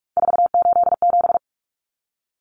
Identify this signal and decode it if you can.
Morse code, heard as a keyed tone, about 44 wpm, 700 Hz. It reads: 487
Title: Numbers - 3-Digit 45wpm